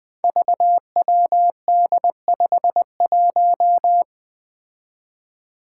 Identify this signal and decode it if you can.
Morse code, heard as a keyed tone, about 21 wpm, 700 Hz. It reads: VWD51
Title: Five Challenging Letter-Number Combinations 20wpm